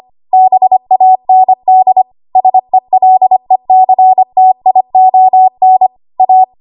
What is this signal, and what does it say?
Morse code, heard as a keyed tone, about 25 wpm, 750 Hz. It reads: BAND SELECTION A
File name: Archive / 25 WPM / 25WPM